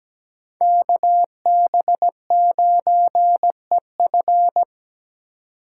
Morse code, heard as a keyed tone, about 17 wpm, 700 Hz. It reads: KB9EF
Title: USA Call Signs - 5 Characters - 8x Character Spacing - 17wpm